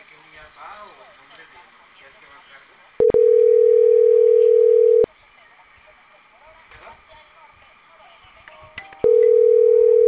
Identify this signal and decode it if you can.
Ringback tone (North America), 440 + 480 Hz, with speech.
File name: CENTRAL CAEA / out agcaea02 - .11796